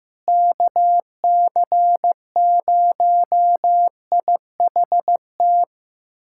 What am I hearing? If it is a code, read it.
Morse code, heard as a keyed tone, about 16 wpm, 700 Hz. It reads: KC0IHT
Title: USA Callsign DE Callsign - 6 Characters - 15wpm